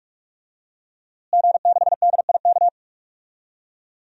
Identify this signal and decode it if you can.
Morse code, heard as a keyed tone, about 44 wpm, 700 Hz. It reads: G6DIK
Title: DXCC-100 Call Signs - 5 Characters - Repeated 3x - 45wpm